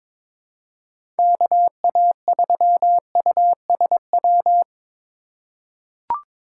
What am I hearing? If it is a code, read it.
Morse code, heard as a keyed tone, about 22 wpm, 700 Hz. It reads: KA3USW
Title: USA Call Signs - 6 Characters - 3x Character Spacing - 22wpm